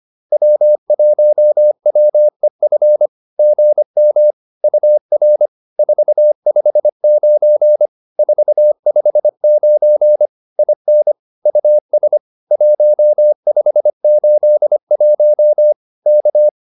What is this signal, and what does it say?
Morse code, heard as a keyed tone, about 25 wpm, 600 Hz. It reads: W1WEF GM UR 459 459 IN US 1581 K
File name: POTA306a 25